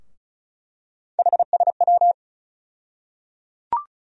Morse code, heard as a keyed tone, about 34 wpm, 700 Hz. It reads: HSW
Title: USA Call Sign Postfixes - 3 Letters 35wpm